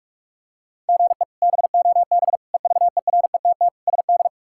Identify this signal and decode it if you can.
Morse code, heard as a keyed tone, about 46 wpm, 700 Hz. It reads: GE BOB EVERETT SD
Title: K1USN SST QSO Practice - 45wpm